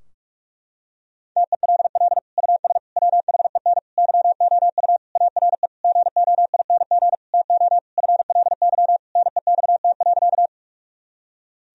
Morse code, heard as a keyed tone, about 41 wpm, 700 Hz. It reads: TELL US WHEN YOU ARE GOING TO FLY NEXT.
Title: Sentences from Top 1000 Words - Review Part 2 - 45wpm